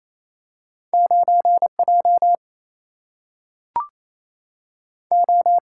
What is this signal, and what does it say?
Morse code, heard as a keyed tone, about 28 wpm, 700 Hz. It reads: 9 J O
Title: Single-Letter Single-Number 28wpm